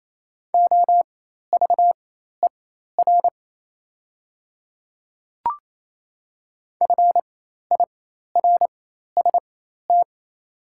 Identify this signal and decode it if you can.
Morse code, heard as a keyed tone, about 28 wpm, spaced as for 13 wpm, 700 Hz. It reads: OVER FIRST
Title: 4x Character Spacing - Top 100 Words - 28wpm